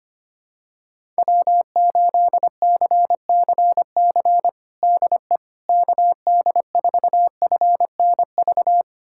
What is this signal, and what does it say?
Morse code, heard as a keyed tone, about 25 wpm, 700 Hz. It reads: W8CCC DE KD4FNV
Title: USA Callsign DE Callsign 25wpm